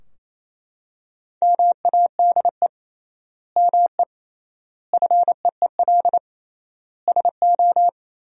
Morse code, heard as a keyed tone, about 28 wpm, 700 Hz. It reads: MADE ME FEEL SO
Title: 3x Word-Spacing - Sentences from Top 400 Words - 28wpm